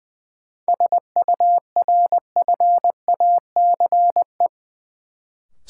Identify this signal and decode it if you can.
Morse code, heard as a keyed tone, about 20 wpm, 700 Hz. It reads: SURFACE
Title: Sentences from Top 600 Words - New Word Drill - 20wpm